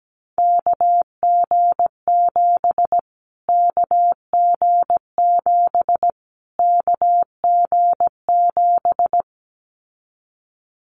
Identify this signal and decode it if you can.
Morse code, heard as a keyed tone, about 17 wpm, 700 Hz. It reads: KG7 KG7 KG7
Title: Callsign Prefixes for Top 25 DXCC - Repeated 3x - 17wpm